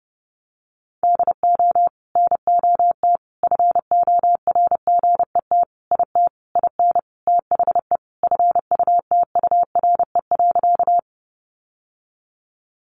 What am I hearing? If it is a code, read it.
Morse code, heard as a keyed tone, about 30 wpm, 700 Hz. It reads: DO NOT FORGET IT IN THE FUTURE.
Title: Sentences from Top 1200 Words 30wpm